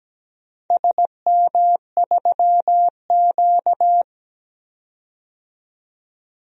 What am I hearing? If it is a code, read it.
Morse code, heard as a keyed tone, about 17 wpm, 700 Hz. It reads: SM3Q
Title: DXCC-100 Call Signs - 4 Characters - 17wpm